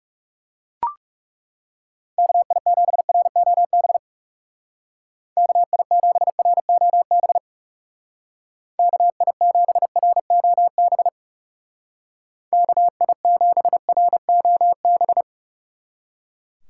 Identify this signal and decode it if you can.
Morse code, heard as a keyed tone, about 36 wpm, 700 Hz. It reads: KI7ROB KI7ROB KI7ROB KI7ROB
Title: Speed Racing - USA 2x3 Call Signs 30wpm